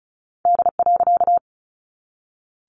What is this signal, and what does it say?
Morse code, heard as a keyed tone, about 34 wpm, 700 Hz. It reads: D.
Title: A Sound Of Thunder 35wpm